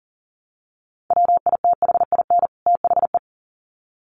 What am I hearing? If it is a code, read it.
Morse code, heard as a keyed tone, about 40 wpm, 700 Hz. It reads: WITHIN THE
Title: Sets of 2 Words 40wpm